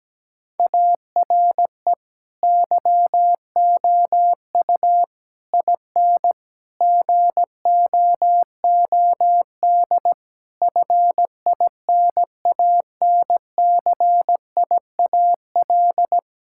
Morse code, heard as a keyed tone, about 17 wpm, 700 Hz. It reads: ARE YOU IN GOOD FINANCIAL
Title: Sentences from Top 1000 Words - Review Part 5 - 17wpm